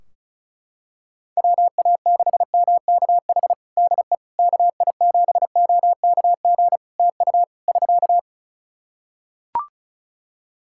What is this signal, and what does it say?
Morse code, heard as a keyed tone, about 36 wpm, 700 Hz. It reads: WA6MKH DE KI7OKG TU <SK>
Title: Practice QSO - Exchange 4 - Ending - 35wpm